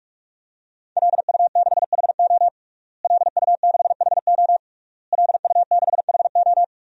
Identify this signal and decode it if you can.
Morse code, heard as a keyed tone, about 41 wpm, 700 Hz. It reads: LU6HO LU6HO LU6HO
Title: DXCC-100 Call Signs - 5 Characters - Repeated 3x - 45wpm